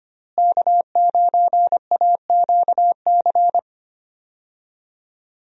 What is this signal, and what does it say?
Morse code, heard as a keyed tone, about 25 wpm, 700 Hz. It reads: K9AQC
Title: USA Call Signs - 5 Characters - Repeated 2x - 25wpm